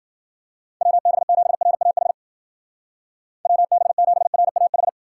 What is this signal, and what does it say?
Morse code, heard as a keyed tone, about 60 wpm, 700 Hz. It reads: WB7RAH WB7RAH